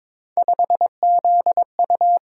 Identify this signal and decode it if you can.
Morse code, heard as a keyed tone, about 23 wpm, 700 Hz. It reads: 5ZU
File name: practice / 22wpm / Three Letter-Number Combinations - 22wpm